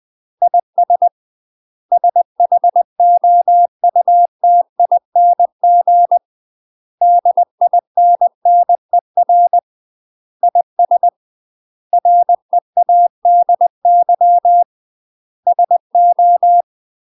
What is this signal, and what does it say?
Morse code, heard as a keyed tone, about 20 wpm, 700 Hz. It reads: IS SHOUTING DINNER IS READY SO
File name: UK MASTER QSOs TRAINING / 20WPM / UK MASTER QSOs 0041